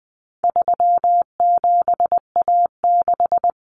Morse code, heard as a keyed tone, about 20 wpm, 700 Hz. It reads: 37A6
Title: Common Cut Numbers - 4-Digit 20wpm